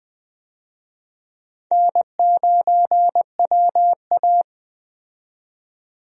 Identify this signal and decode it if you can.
Morse code, heard as a keyed tone, about 20 wpm, 700 Hz. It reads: N9WA
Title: USA Call Signs - 4 Characters - 20wpm